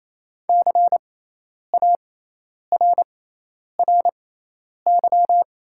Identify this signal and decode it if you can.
Morse code, heard as a keyed tone, about 28 wpm, spaced as for 10 wpm, 700 Hz. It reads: CARRY
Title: 6x Character Spacing - Top 300 Words - 28wpm